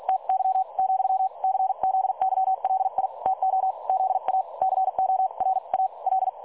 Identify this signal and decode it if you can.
Morse code, heard as a keyed tone, about 48 wpm, spaced as for 39 wpm, 760 Hz. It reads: TY , CZYLI TO ZACORA L
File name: dlazaawansowanych / nagraniazpasma / websdr recording 3565.0kHz